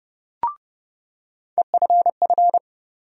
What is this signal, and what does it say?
Morse code, heard as a keyed tone, about 30 wpm, 700 Hz. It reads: EFF
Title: USA Call Sign Postfixes - 3 Letters 30wpm